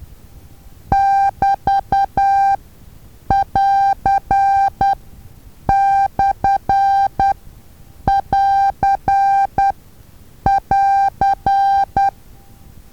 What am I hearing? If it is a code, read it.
Morse code, heard as a keyed tone, about 10 wpm, spaced as for 7 wpm, 790 Hz. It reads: <BT><AR>/<AR><AR>
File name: Taviro HA6VA A